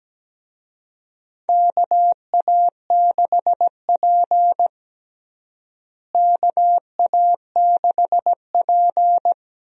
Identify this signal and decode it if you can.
Morse code, heard as a keyed tone, about 17 wpm, 700 Hz. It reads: KA6P KA6P
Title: USA Call Signs - 4 Characters - Repeated 2x - 17wpm